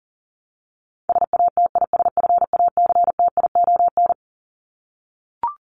Morse code, heard as a keyed tone, about 40 wpm, 700 Hz. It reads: SATISFACTION
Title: Top 1751-2000 Words 40wpm